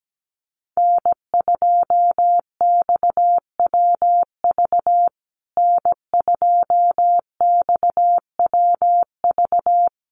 Morse code, heard as a keyed tone, about 17 wpm, 700 Hz. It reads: N2XWV N2XWV
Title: USA Call Signs Repeated 3x 17wpm